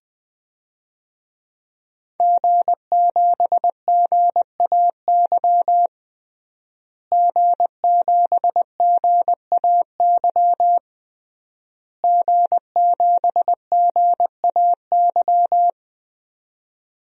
Morse code, heard as a keyed tone, about 21 wpm, 700 Hz. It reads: G7GAY G7GAY G7GAY
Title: DXCC-100 Call Signs - 5 Characters - Repeated 3x - 20wpm